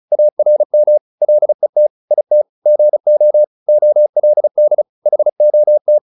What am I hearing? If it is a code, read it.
Morse code, heard as a keyed tone, about 34 wpm, 600 Hz. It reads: ARM LET IT GO OLD HOT